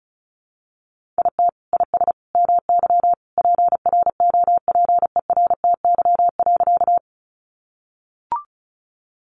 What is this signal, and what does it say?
Morse code, heard as a keyed tone, about 34 wpm, 700 Hz. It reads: IT IS MY PROPERTY.
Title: Sentences from Top 1000 Words 35wpm